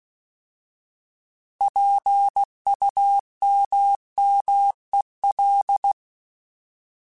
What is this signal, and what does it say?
Morse code, heard as a keyed tone, about 16 wpm, 800 Hz. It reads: PUMMEL
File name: Alphabetical / Alphabetical mp3s / Ps @15